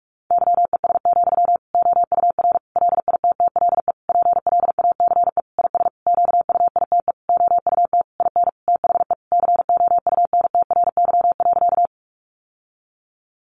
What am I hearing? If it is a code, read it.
Morse code, heard as a keyed tone, about 44 wpm, 700 Hz. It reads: YES, OUR LITTLE PLACE IS QUITE OUT IN THE COUNTRY.
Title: Sentences from Top 700 Words 45wpm